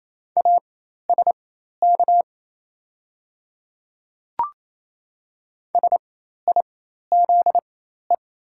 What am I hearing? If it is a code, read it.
Morse code, heard as a keyed tone, about 28 wpm, spaced as for 13 wpm, 700 Hz. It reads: ASK SIZE
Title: 4x Character Spacing - Top 500 Words - 28wpm